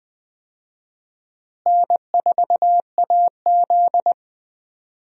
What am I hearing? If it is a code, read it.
Morse code, heard as a keyed tone, about 20 wpm, 700 Hz. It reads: N4AZ
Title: USA Call Signs - 4 Characters - Repeated 2x - 20wpm